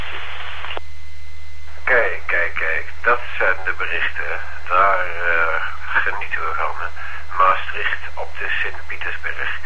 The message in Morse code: none